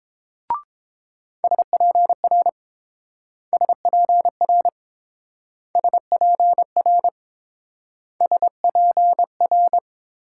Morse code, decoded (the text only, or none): SPR SPR SPR SPR